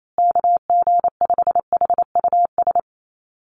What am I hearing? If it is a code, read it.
Morse code, heard as a keyed tone, about 28 wpm, 700 Hz. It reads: KG5HUS
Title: USA Call Signs Repeated 3x 28wpm